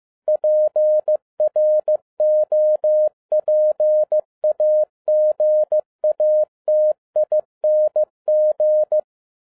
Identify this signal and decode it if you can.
Morse code, heard as a keyed tone, about 15 wpm, 610 Hz. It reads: PROPAGATING